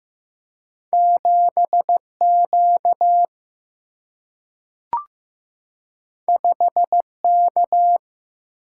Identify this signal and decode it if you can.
Morse code, heard as a keyed tone, about 15 wpm, 700 Hz. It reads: 7Q 5K
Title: Single-Letter Single-Number 15wpm